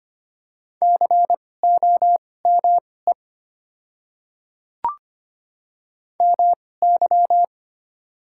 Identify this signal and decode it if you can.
Morse code, heard as a keyed tone, about 25 wpm, spaced as for 18 wpm, 700 Hz. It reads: COME MY
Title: 2x Character Spacing - Top 300 Words - 25wpm